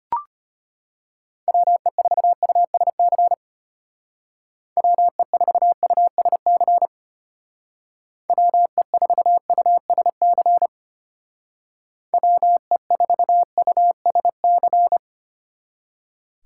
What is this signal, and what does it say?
Morse code, heard as a keyed tone, about 32 wpm, 700 Hz. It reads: WE4USC WE4USC WE4USC WE4USC